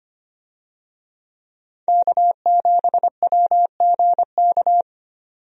Morse code, heard as a keyed tone, about 25 wpm, 700 Hz. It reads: K7WGK
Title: USA Call Signs - 5 Characters - 25wpm